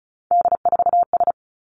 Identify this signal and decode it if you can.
Morse code, heard as a keyed tone, about 36 wpm, 700 Hz. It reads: D4S
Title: Three Challenging Letter-Number Combinations 35wpm